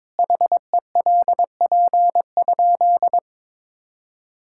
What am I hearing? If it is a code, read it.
Morse code, heard as a keyed tone, about 22 wpm, 700 Hz. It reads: HELP?
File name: practice / 22wpm / Sentences with 4-Word Phrases - 22wpm